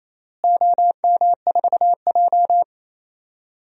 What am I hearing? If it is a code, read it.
Morse code, heard as a keyed tone, about 28 wpm, 700 Hz. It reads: OM4J
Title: DXCC-100 Call Signs - 4 Characters - Repeated 3x - 28wpm